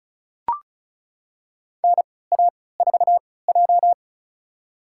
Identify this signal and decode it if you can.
Morse code, heard as a keyed tone, about 36 wpm, spaced as for 20 wpm, 700 Hz. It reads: NA4J
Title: USA Call Signs - 4 Characters - 3x Character Spacing - 35wpm